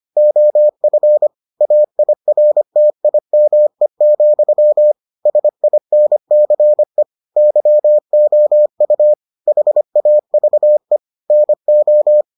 Morse code, heard as a keyed tone, about 25 wpm, 600 Hz. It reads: OF AIRTIME, SINCE YOU HAVE NO